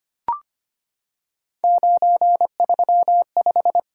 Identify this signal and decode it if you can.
Morse code, heard as a keyed tone, about 25 wpm, 700 Hz. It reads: 935
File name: Three Letter-Number Combinations - 25wpm